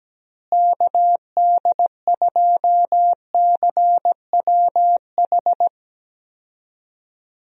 Morse code, heard as a keyed tone, about 17 wpm, 700 Hz. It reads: KD2CWH
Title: USA Call Signs - 6 Characters - Repeated 3x - 17wpm